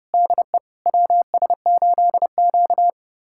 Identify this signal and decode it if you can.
Morse code, heard as a keyed tone, about 30 wpm, 700 Hz. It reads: DE WS8Q
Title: USA Callsign DE Callsign - 4 Characters - 30wpm